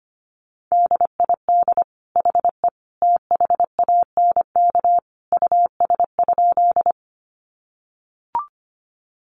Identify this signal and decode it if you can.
Morse code, heard as a keyed tone, about 25 wpm, 700 Hz. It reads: DID HE THANK US?
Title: Sentences from Top 600 Words 25wpm